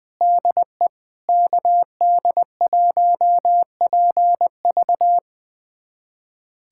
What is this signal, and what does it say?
Morse code, heard as a keyed tone, about 20 wpm, 700 Hz. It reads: DE KD1PV
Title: USA Callsign DE Callsign 20wpm